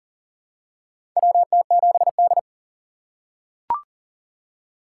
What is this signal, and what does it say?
Morse code, heard as a keyed tone, about 40 wpm, 700 Hz. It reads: WT7D
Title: USA Call Signs - 4 Characters - 4x Character Spacing - 40wpm